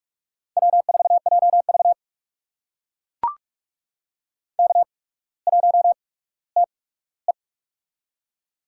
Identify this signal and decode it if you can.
Morse code, heard as a keyed tone, about 44 wpm, spaced as for 19 wpm, 690 Hz. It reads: W4JV K1TE
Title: USA Call Signs - 4 Characters - 8x Character Spacing - 45wpm